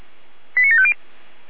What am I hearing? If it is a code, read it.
Five-tone selective call (ZVEI), heard as quick runs of five tones, tones of about 80 ms. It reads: 89750